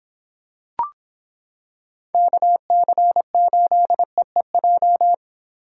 Morse code, heard as a keyed tone, about 26 wpm, 700 Hz. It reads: KC8EEJ